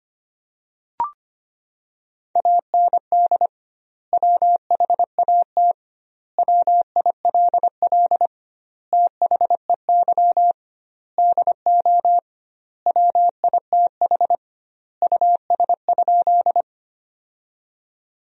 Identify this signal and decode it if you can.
Morse code, heard as a keyed tone, about 25 wpm, 700 Hz. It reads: AND WHAT WILL THEY DO WITH US?
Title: 2x Word-Spacing - Sentences from Top 200 Words - 25wpm